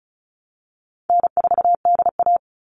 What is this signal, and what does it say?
Morse code, heard as a keyed tone, about 34 wpm, 700 Hz. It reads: N4DA
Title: NA SOTA Chaser Callsigns 35wpm